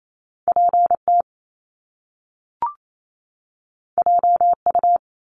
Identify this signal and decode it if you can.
Morse code, heard as a keyed tone, about 28 wpm, 700 Hz. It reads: PT JU
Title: Top Two Letter Combinations in Words 28wpm